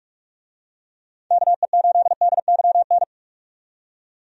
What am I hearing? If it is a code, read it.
Morse code, heard as a keyed tone, about 45 wpm, 700 Hz. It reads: KE8DYN
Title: USA Call Signs - 6 Characters - 45wpm